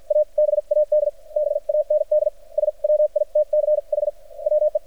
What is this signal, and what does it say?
Morse code, heard as a keyed tone, about 53 wpm, spaced as for 44 wpm, 600 Hz. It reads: A BAD BAND SWITCH WE
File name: archived / CRSnet